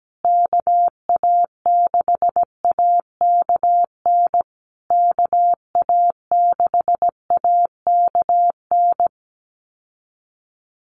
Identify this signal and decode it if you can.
Morse code, heard as a keyed tone, about 17 wpm, 700 Hz. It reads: KA6AKN KA6AKN
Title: USA Call Signs Repeated 3x 17wpm